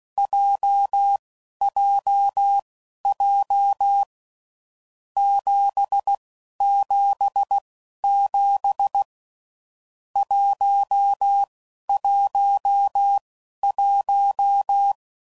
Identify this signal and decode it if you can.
Morse code, heard as a keyed tone, about 16 wpm, spaced as for 12 wpm, 800 Hz. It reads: JJJ 777 111
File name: Groups / Basic Letter Groups / JZ17